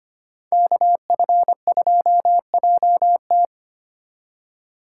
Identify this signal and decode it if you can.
Morse code, heard as a keyed tone, about 32 wpm, 700 Hz. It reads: KF2JT